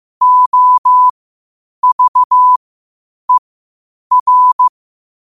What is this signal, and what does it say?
Morse code, heard as a keyed tone, about 15 wpm, spaced as for 8 wpm, 1000 Hz. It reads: OVER
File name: qotdfiles / 8wpm